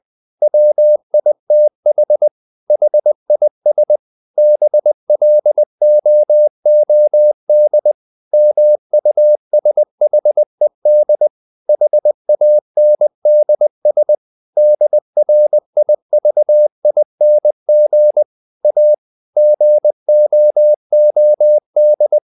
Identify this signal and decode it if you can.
Morse code, heard as a keyed tone, about 20 wpm, 600 Hz. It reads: WITH HIS BLOOD MUSHED HANDS DRIVING A GOOD
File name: the old man and the sea part 2 20wpm